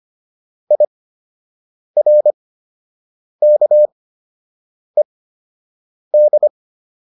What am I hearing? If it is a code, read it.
Morse code, heard as a keyed tone, about 25 wpm, spaced as for 7 wpm, 600 Hz. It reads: IRKED